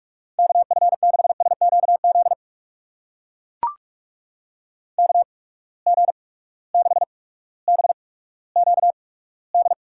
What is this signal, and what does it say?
Morse code, heard as a keyed tone, about 42 wpm, spaced as for 19 wpm, 700 Hz. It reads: KF6SQZ KG6BQD
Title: USA Call Signs - 6 Characters - 8x Character Spacing - 45wpm